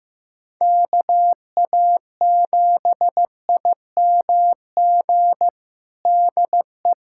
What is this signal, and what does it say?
Morse code, heard as a keyed tone, about 15 wpm, 700 Hz. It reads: KA7IMG DE
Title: USA Callsign DE Callsign - 6 Characters - 15wpm